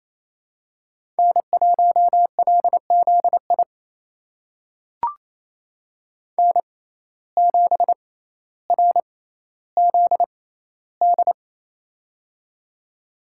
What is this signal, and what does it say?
Morse code, heard as a keyed tone, about 28 wpm, spaced as for 14 wpm, 700 Hz. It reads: N1LZI N7RZD
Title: USA Call Signs - 5 Characters - 6x Character Spacing - 28wpm